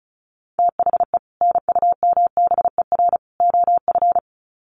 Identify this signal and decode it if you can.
Morse code, heard as a keyed tone, about 34 wpm, 700 Hz. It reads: THE NUMBER OF
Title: Sets of 3 Words 35wpm